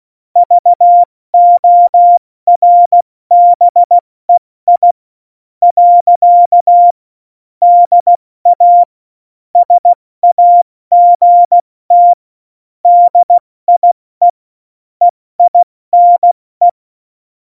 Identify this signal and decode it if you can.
Morse code, heard as a keyed tone, about 16 wpm, 700 Hz. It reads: VORBEI . DA SAGT DIE EINE
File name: Morsekurs / Witze / Witz 26. Juni 2018, 70 F80